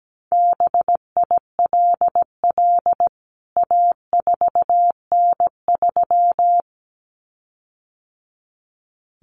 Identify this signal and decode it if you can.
Morse code, heard as a keyed tone, about 17 wpm, 700 Hz. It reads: BILL A4N3
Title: CWT Exchange 17wpm